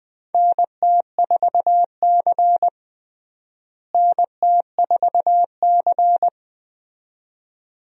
Morse code, heard as a keyed tone, about 21 wpm, 700 Hz. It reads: NT4C NT4C
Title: USA Call Signs - 4 Characters - Repeated 2x - 20wpm